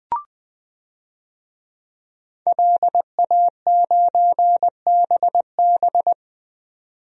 Morse code, heard as a keyed tone, about 20 wpm, 700 Hz. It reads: LA9BB
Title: DXCC-100 Call Signs - 5 Characters - 20wpm